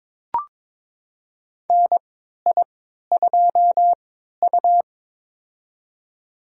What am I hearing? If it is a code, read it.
Morse code, heard as a keyed tone, about 22 wpm, spaced as for 13 wpm, 700 Hz. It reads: NI2U
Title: USA Call Signs - 4 Characters - 3x Character Spacing - 22wpm